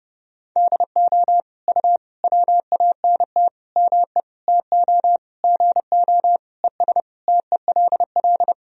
Morse code, heard as a keyed tone, about 32 wpm, 700 Hz. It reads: DO U WANT ME TO GO ES TELL
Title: Sentences from Top 200 Words and Abbreviations 30wpm